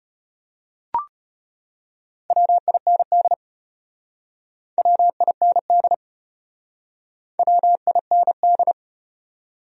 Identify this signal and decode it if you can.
Morse code, heard as a keyed tone, about 34 wpm, 700 Hz. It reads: WIND WIND WIND